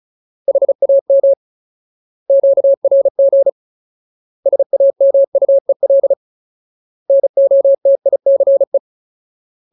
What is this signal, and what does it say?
Morse code, heard as a keyed tone, about 34 wpm, 550 Hz. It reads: HAM QRG SAMUEL NOTICE
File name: Ultimate Word List / 35WPM Ultimate